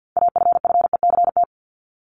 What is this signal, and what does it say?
Morse code, heard as a keyed tone, about 48 wpm, 700 Hz. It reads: AFFECT
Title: Top 751-1000 Words 50wpm